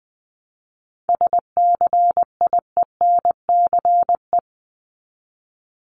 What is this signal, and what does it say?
Morse code, heard as a keyed tone, about 20 wpm, 700 Hz. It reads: SCIENCE
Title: Top 500 Words 20wpm